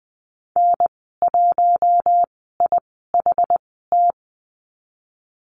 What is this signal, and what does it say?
Morse code, heard as a keyed tone, about 20 wpm, spaced as for 14 wpm, 700 Hz. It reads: N1IHT
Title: 062 Lesson 11g - Letter h - Call Signs 20wpm